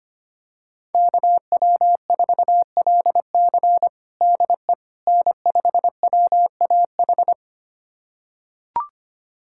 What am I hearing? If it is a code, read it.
Morse code, heard as a keyed tone, about 25 wpm, 700 Hz. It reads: KW4LC DE N5WAH
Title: USA Callsign DE Callsign - 5 Characters - 25wpm